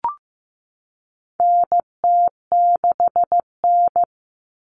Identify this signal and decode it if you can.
Morse code, heard as a keyed tone, about 15 wpm, 700 Hz. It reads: NT6N
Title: All Cut Numbers - 4-Digit 15wpm